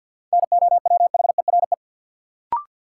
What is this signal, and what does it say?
Morse code, heard as a keyed tone, about 53 wpm, 700 Hz. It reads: NOWHERE
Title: Sets of 5 Words Encore - New Word Drill - 50wpm